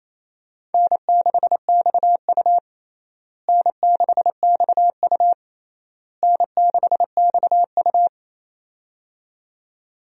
Morse code, heard as a keyed tone, about 28 wpm, 700 Hz. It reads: N6XU N6XU N6XU
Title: USA Call Signs - 4 Characters - Repeated 3x - 28wpm